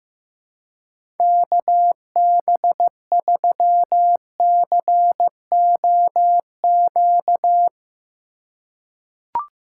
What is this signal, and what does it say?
Morse code, heard as a keyed tone, about 15 wpm, 700 Hz. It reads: KB3COQ
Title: USA Call Signs - 6 Characters - 3x Character Spacing - 15wpm